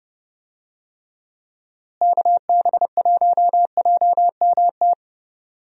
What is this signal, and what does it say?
Morse code, heard as a keyed tone, about 30 wpm, 700 Hz. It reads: KB1JMT
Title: USA Call Signs - 6 Characters - Repeated 3x - 30wpm